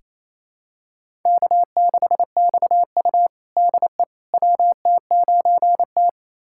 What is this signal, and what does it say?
Morse code, heard as a keyed tone, about 28 wpm, 700 Hz. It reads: K6XU DE WT9T
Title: USA Callsign DE Callsign - 4 Characters - 28wpm